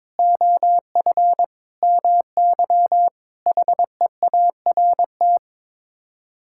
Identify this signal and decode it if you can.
Morse code, heard as a keyed tone, about 21 wpm, 700 Hz. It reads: OF MY HEART
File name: practice / 22wpm / Sets of 6 Words - 22wpm